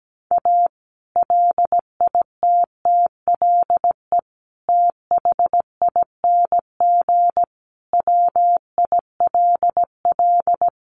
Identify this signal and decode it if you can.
Morse code, heard as a keyed tone, about 17 wpm, 700 Hz. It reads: A LITTLE THING WILL
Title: Sentences from Top 1500 Words 17wpm